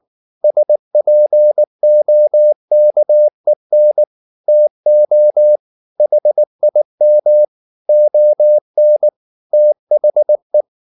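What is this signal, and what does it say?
Morse code, heard as a keyed tone, about 19 wpm, 600 Hz. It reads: SPOKEN TO HIM ON THE
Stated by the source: Morse code at 19 wpm from CW med Gnister, reading Ken Follett's novel Never in English.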